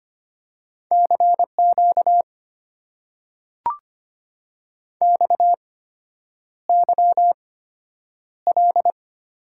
Morse code, heard as a keyed tone, about 25 wpm, spaced as for 9 wpm, 700 Hz. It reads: CQ XYL
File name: Top 90 QSO Elements - 8x Character Spacing - 25wpm